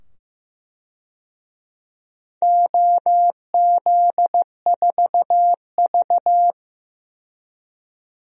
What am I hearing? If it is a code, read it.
Morse code, heard as a keyed tone, about 16 wpm, 700 Hz. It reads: OZ4V
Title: DXCC-100 Call Signs - 4 Characters - 15wpm